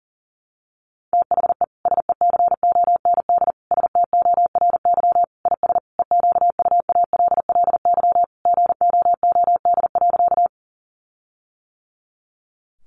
Morse code, heard as a keyed tone, about 40 wpm, 700 Hz. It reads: THE SECOND STORY IS EQUALLY GOOD.
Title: Sentences from Top 1400 Words 40wpm